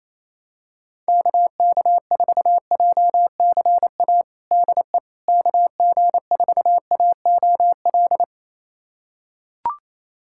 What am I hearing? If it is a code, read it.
Morse code, heard as a keyed tone, about 28 wpm, 700 Hz. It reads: KK4JCA DE KG4AOL
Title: USA Callsign DE Callsign - 6 Characters - 28wpm